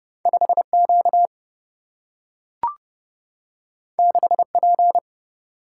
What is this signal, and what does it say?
Morse code, heard as a keyed tone, about 30 wpm, 700 Hz. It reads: 5Q 6P
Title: Single-Letter Single-Number 30wpm